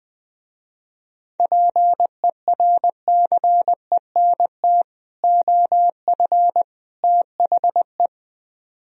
Morse code, heard as a keyed tone, about 21 wpm, 700 Hz. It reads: PERCENT OF THE